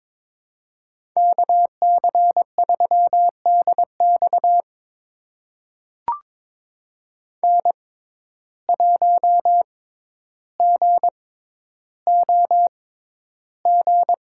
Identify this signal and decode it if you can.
Morse code, heard as a keyed tone, about 21 wpm, spaced as for 11 wpm, 700 Hz. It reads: KC3DX N1GOG
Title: USA Call Signs - 5 Characters - 6x Character Spacing - 22wpm